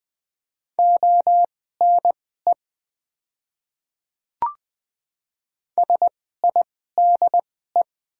Morse code, heard as a keyed tone, about 20 wpm, spaced as for 14 wpm, 690 Hz. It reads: ONE SIDE
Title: 2x Character Spacing - Top 100 Words - 20wpm